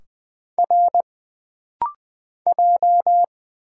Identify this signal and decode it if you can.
Morse code, heard as a keyed tone, about 20 wpm, 700 Hz. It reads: R J
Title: Single Letters - Rapid-Fire 20wpm